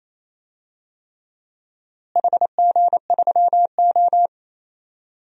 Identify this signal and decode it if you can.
Morse code, heard as a keyed tone, about 28 wpm, 700 Hz. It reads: HG3O
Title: DXCC-100 Call Signs - 4 Characters - Repeated 3x - 28wpm